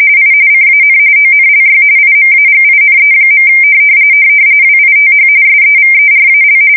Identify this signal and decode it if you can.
RTTY, about 45.45 baud. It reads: I CAN ONLY IMAGINE. WITH SEVERAL
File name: fsk170 / USS-New-Jersey